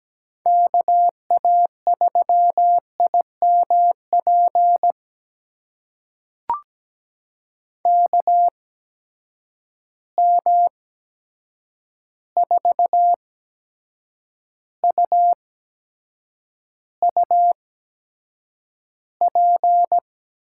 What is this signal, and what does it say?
Morse code, heard as a keyed tone, about 17 wpm, spaced as for 7 wpm, 700 Hz. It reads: KA3IMP KM4UUP
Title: USA Call Signs - 6 Characters - 8x Character Spacing - 17wpm